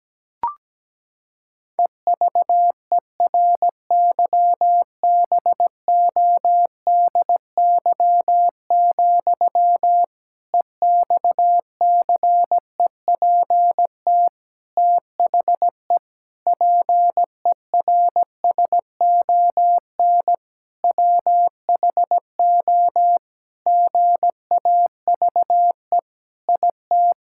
Morse code, heard as a keyed tone, about 17 wpm, 700 Hz. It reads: EVERYBODY, EXCEPT THE PERSON WHO GAVE IT
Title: Sentences from Top 1600 Words 17wpm